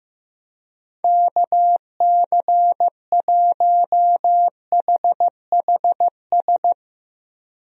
Morse code, heard as a keyed tone, about 15 wpm, 700 Hz. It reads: KC1HHS
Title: USA Call Signs - 6 Characters - Repeated 3x - 15wpm